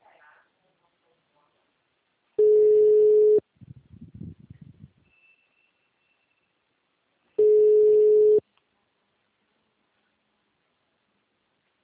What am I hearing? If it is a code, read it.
Ringback tone (Europe, 425 Hz).